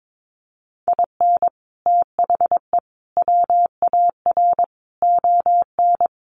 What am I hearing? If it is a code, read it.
Morse code, heard as a keyed tone, about 22 wpm, 700 Hz. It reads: IN THE WAR ON
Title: Sets of 5 Words Encore2 22wpm